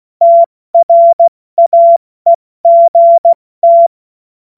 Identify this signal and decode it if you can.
Morse code, heard as a keyed tone, about 16 wpm, spaced as for 14 wpm, 680 Hz. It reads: TRAEGT
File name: Witz-07-Oktober-2025-70-F80